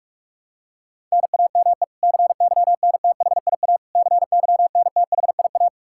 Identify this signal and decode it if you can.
Morse code, heard as a keyed tone, about 46 wpm, 700 Hz. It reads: NAME CYNTHIA CYNTHIA
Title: K1USN SST QSO Practice 45wpm